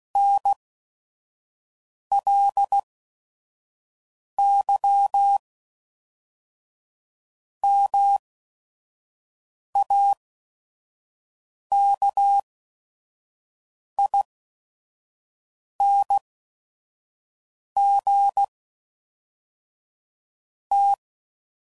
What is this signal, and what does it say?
Morse code, heard as a keyed tone, about 16 wpm, spaced as for 5 wpm, 800 Hz. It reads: NLY MAKING T
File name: CW Code Groups @5WPM / Technical Subjects / AND Gate